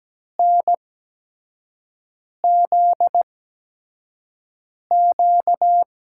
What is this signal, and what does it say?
Morse code, heard as a keyed tone, about 17 wpm, spaced as for 5 wpm, 700 Hz. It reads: NZQ
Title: USA Call Signs - 5 Characters - 8x Character Spacing - 17wpm